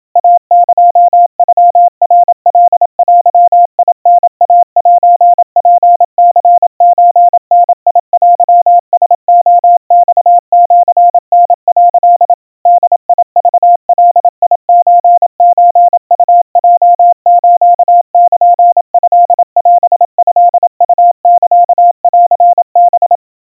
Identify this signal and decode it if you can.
Morse code (Japanese Wabun code), heard as a keyed tone, about 27 wpm, 700 Hz. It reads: イエノナカデタイセツニソダテラレマシタ」 ボクガソソウヲスルトオトウサンハ
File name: kawa / Morse sample / kamyu 100 cpm 700 Hz Crispy